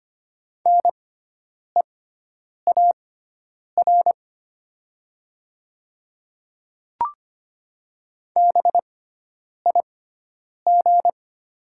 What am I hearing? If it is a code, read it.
Morse code, heard as a keyed tone, about 25 wpm, spaced as for 9 wpm, 700 Hz. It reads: NEAR BIG
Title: 6x Character Spacing - Top 300 Words - 25wpm